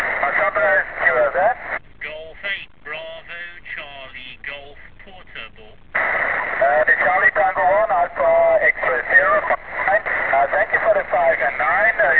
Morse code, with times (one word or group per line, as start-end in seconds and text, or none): none